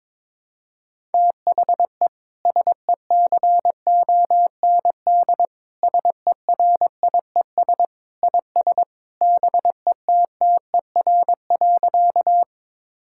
Morse code, heard as a keyed tone, about 22 wpm, 700 Hz. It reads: THE SECOND SERIES IS BETTER.
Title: Sentences from Top 1000 Words - Review Part 5 - 22wpm